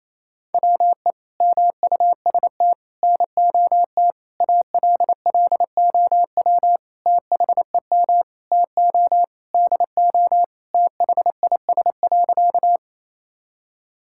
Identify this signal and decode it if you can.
Morse code, heard as a keyed tone, about 28 wpm, 700 Hz. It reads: WE MUST NOT ALLOW THEM TO DO THIS.
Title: Sentences from Top 1000 Words - Review Part 5 - 28wpm